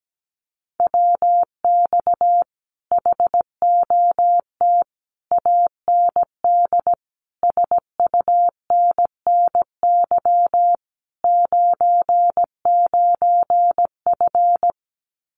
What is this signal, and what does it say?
Morse code, heard as a keyed tone, about 17 wpm, 700 Hz. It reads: WX HOT AND SUNNY 99F
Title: Common QSO Phrases Encore 17wpm